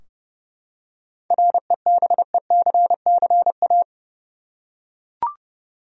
Morse code, heard as a keyed tone, about 31 wpm, 700 Hz. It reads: REBECCA
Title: USA Common Names - 30wpm